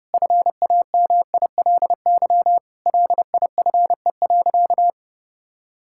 Morse code, heard as a keyed tone, about 32 wpm, 700 Hz. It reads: FAMILY LIFE.
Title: Sentences from Top 1000 Words - Review Part 5 - 30wpm